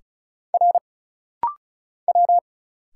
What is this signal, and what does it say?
Morse code, heard as a keyed tone, about 36 wpm, 700 Hz. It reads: R W
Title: Single Letters - Rapid-Fire 35wpm